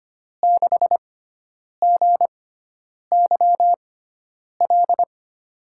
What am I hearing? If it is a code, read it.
Morse code, heard as a keyed tone, about 25 wpm, spaced as for 9 wpm, 700 Hz. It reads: 6GYL
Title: USA Call Signs - 6 Characters - 6x Character Spacing - 25wpm